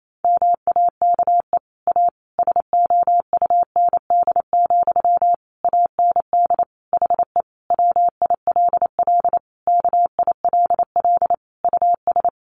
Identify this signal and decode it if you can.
Morse code, heard as a keyed tone, about 28 wpm, 700 Hz. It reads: MAKE A SOUND, AND HE WILL KILL US
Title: Sentences from Top 1400 Words 28wpm